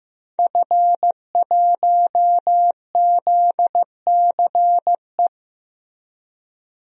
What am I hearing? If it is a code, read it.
Morse code, heard as a keyed tone, about 15 wpm, 700 Hz. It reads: F1ZCE
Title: DXCC-100 Call Signs - 5 Characters - Repeated 2x - 15wpm